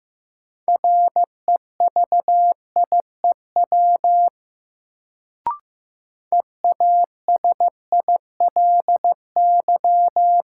Morse code, heard as a keyed tone, about 15 wpm, 700 Hz. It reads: REVIEW EASILY
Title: Sentences from Top 900 Words - New Word Drill - 15wpm